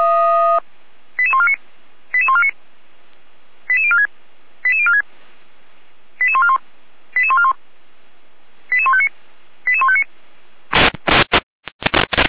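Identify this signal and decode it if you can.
Five-tone selective call (ZVEI), heard as quick runs of five tones, tones of about 70 ms. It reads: 88149 88149 88046 88046 88141 88141 88159 88159
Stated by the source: ZVEI five-tone selective call received by SDR, logged by the Feuerwehr Orlamünde fire brigade server.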